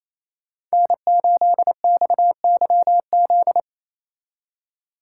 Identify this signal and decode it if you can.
Morse code, heard as a keyed tone, about 36 wpm, 700 Hz. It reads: N8XYZ